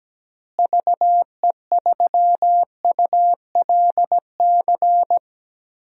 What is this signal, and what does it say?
Morse code, heard as a keyed tone, about 17 wpm, 700 Hz. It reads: VE3ULC
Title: DXCC-100 Call Signs - 6 Characters - Repeated 2x - 17wpm